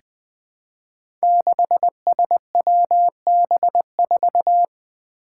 Morse code, heard as a keyed tone, about 20 wpm, 700 Hz. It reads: 6SWB4
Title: Five Challenging Letter-Number Combinations 20wpm